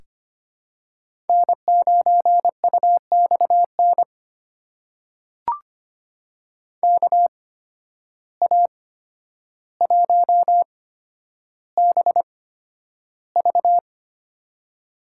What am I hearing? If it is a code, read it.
Morse code, heard as a keyed tone, about 25 wpm, spaced as for 11 wpm, 700 Hz. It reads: N9UXN KA1BV
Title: USA Call Signs - 8x Character Spacing - 25wpm